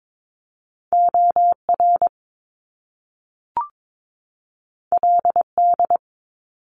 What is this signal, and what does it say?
Morse code, heard as a keyed tone, about 22 wpm, 700 Hz. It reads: OR LD
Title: Top Two Letter Combinations in Words 22wpm